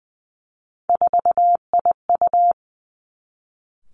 Morse code, heard as a keyed tone, about 20 wpm, 700 Hz. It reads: 4IU
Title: Three Challenging Letter-Number Combinations 20wpm